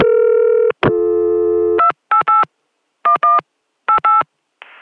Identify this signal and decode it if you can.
DTMF, heard as telephone touch tones, tones of about 110 ms. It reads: 288 11 88